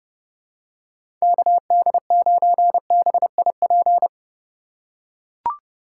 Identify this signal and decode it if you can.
Morse code, heard as a keyed tone, about 32 wpm, 700 Hz. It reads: KD9BIP
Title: USA Call Signs - 6 Characters - 3x Character Spacing - 30wpm